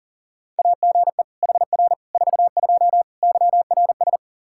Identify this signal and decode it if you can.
Morse code, heard as a keyed tone, about 40 wpm, 700 Hz. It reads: AGE HR 42 YRS
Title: Practice QSO - Exchange 3 - More Info - 40wpm